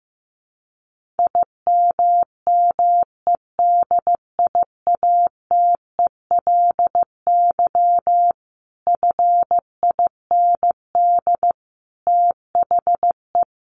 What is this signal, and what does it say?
Morse code, heard as a keyed tone, about 15 wpm, 700 Hz. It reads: IMMEDIATELY FIND THE
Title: Sentences from Top 1200 Words 15wpm